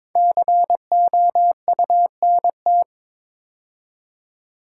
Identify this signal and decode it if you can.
Morse code, heard as a keyed tone, about 22 wpm, 700 Hz. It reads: COUNT
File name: 5 Letter Words - First 3 Letters Given - 22wpm